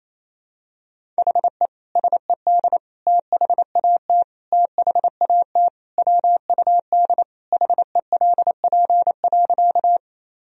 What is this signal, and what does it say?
Morse code, heard as a keyed tone, about 27 wpm, 700 Hz. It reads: HE SED THAT THAT WUD HELP.
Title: Sentences from Top 200 Words and Abbreviations 28wpm